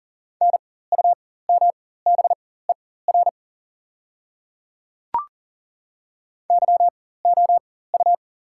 Morse code, heard as a keyed tone, about 40 wpm, spaced as for 19 wpm, 690 Hz. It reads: NUMBER YOU
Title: 4x Character Spacing - Top 100 Words - 40wpm